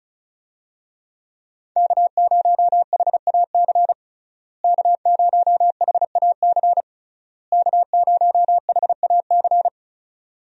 Morse code, heard as a keyed tone, about 36 wpm, 700 Hz. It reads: K0HAC K0HAC K0HAC
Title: USA Call Signs - 5 Characters - Repeated 3x - 35wpm